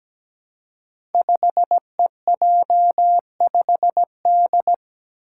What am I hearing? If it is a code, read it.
Morse code, heard as a keyed tone, about 17 wpm, 700 Hz. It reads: 5EJ5D
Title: Five Challenging Letter-Number Combinations 17wpm